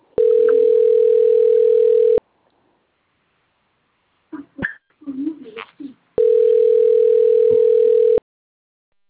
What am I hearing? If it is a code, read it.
Ringback tone (North America), 440 + 480 Hz.
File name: rg-1600-unknown- - - .87766